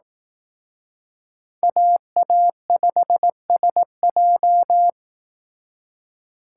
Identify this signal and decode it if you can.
Morse code, heard as a keyed tone, about 18 wpm, 700 Hz. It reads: AA5SJ
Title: Speed Racing - USA Call Signs 15wpm